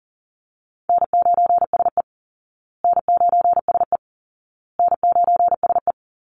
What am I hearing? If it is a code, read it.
Morse code, heard as a keyed tone, about 40 wpm, 700 Hz. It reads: N9SE N9SE N9SE
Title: Easy Call Signs Repeated 3x - 40wpm